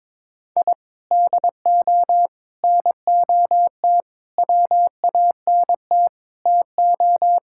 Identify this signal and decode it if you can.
Morse code, heard as a keyed tone, about 22 wpm, 700 Hz. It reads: I DO NOT WANT TO
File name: practice / 22wpm / Sentences with 4-Word Phrases - 22wpm